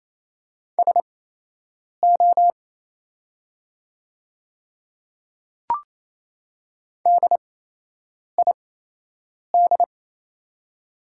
Morse code, heard as a keyed tone, about 28 wpm, spaced as for 8 wpm, 700 Hz. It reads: SO DID
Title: 8x Character Spacing - Top 100 Words - 28wpm